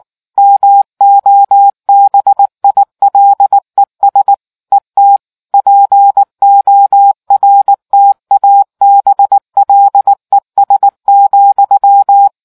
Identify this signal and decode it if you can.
Morse code, heard as a keyed tone, about 19 wpm, 800 Hz. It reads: MOBILES ET PORTABLES,